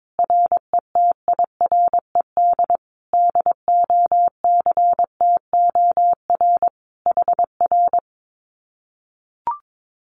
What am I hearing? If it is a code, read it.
Morse code, heard as a keyed tone, about 22 wpm, 700 Hz. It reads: RETIRED DOCTOR HR
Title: Common QSO Phrases Encore 22wpm